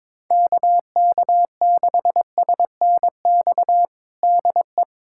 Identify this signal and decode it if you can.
Morse code, heard as a keyed tone, about 22 wpm, 700 Hz. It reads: KK6SNX DE
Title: USA Callsign DE Callsign - 6 Characters - 22wpm